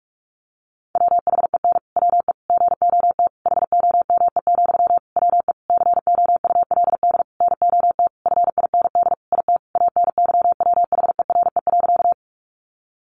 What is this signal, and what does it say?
Morse code, heard as a keyed tone, about 44 wpm, 700 Hz. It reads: WHEN WE GOT HOME, WE COULD NOT FIND IT ANYWHERE.
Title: Sentences from Top 1400 Words 45wpm